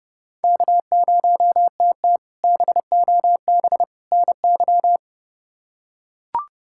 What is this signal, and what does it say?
Morse code, heard as a keyed tone, about 30 wpm, 700 Hz. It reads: K0TT BOB NY
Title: K1USN SST QSO Practice - 30wpm